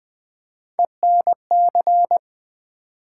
Morse code, heard as a keyed tone, about 20 wpm, 700 Hz. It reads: ENC
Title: Build A Word - 3 Letters At A Time - 9-Letter Words - 20wpm